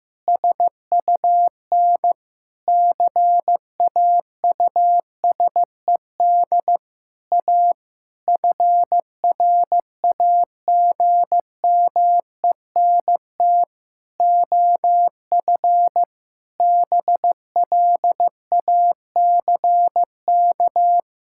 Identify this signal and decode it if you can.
Morse code, heard as a keyed tone, about 14 wpm, 700 Hz. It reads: SUN CAUSED A FRAGMENT OF BLACK
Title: Pangram Challenge v2 - 15wpm